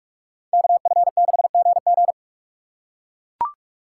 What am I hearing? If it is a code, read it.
Morse code, heard as a keyed tone, about 40 wpm, 700 Hz. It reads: KF6GG